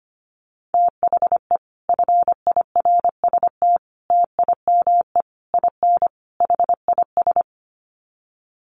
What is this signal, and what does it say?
Morse code, heard as a keyed tone, about 25 wpm, 700 Hz. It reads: THE FIRST TIME IN HIS
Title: Sets of 5 Words Encore2 25wpm